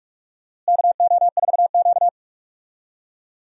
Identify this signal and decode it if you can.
Morse code, heard as a keyed tone, about 46 wpm, 700 Hz. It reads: KO4Q
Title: USA Call Signs - 4 Characters - Repeated 3x - 45wpm